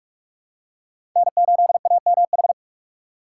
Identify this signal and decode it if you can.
Morse code, heard as a keyed tone, about 47 wpm, 700 Hz. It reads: N8AMH
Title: USA Call Signs - 4x Character Spacing - 45wpm